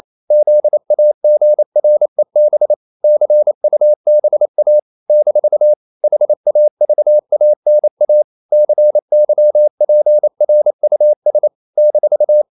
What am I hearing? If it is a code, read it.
Morse code, heard as a keyed tone, about 28 wpm, 600 Hz. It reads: ZAGREB CUBA - HAVANA CYPRUS -
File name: CW / Practice / NationalCapitals28